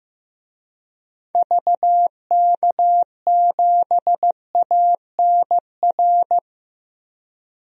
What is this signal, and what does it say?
Morse code, heard as a keyed tone, about 15 wpm, 700 Hz. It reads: VK7ANR
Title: DXCC-100 Call Signs - 6 Characters - Repeated 2x - 15wpm